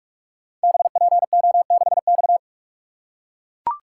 Morse code, heard as a keyed tone, about 44 wpm, 700 Hz. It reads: BPO6X
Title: Code Groups - 5 Letter:Numbers - 45wpm